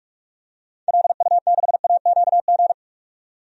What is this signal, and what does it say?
Morse code, heard as a keyed tone, about 45 wpm, 700 Hz. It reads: LU6AQG
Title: DXCC-100 Call Signs - 6 Characters - 45wpm